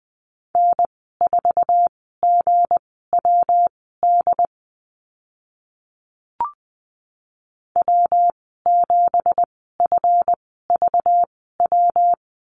Morse code, heard as a keyed tone, about 20 wpm, spaced as for 14 wpm, 700 Hz. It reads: N4GWD W7FVW
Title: 193 Lesson 31g - Letter v - Call Signs 20wpm